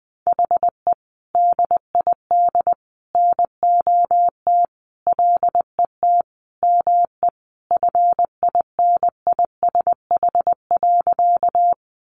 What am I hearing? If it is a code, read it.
Morse code, heard as a keyed tone, about 20 wpm, 700 Hz. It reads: HE DID NOT LET ME FINISH.
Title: Sentences from Top 1200 Words 20wpm